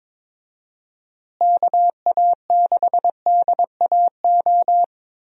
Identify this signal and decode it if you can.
Morse code, heard as a keyed tone, about 21 wpm, 700 Hz. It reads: KA6DAO
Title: USA Call Signs - 6 Characters - Repeated 2x - 22wpm